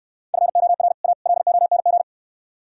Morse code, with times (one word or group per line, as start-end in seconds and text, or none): 0.3-0.9 FOR
1.0-1.1 A
1.3-2.0 LONG